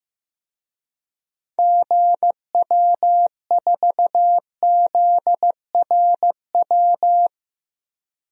Morse code, heard as a keyed tone, about 15 wpm, 700 Hz. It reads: GW4ZRW
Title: DXCC-100 Call Signs - 6 Characters - 15wpm